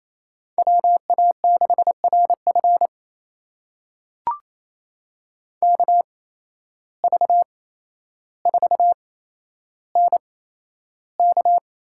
Morse code, heard as a keyed tone, about 28 wpm, spaced as for 12 wpm, 700 Hz. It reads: WA6RF KV4NK
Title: USA Call Signs - 5 Characters - 8x Character Spacing - 28wpm